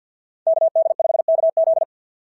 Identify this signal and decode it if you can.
Morse code, heard as a keyed tone, about 50 wpm, 690 Hz. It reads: KD5KZ
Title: SOTA QSO Practice 50wpm